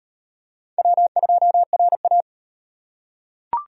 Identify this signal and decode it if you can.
Morse code, heard as a keyed tone, about 34 wpm, 700 Hz. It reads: W2RA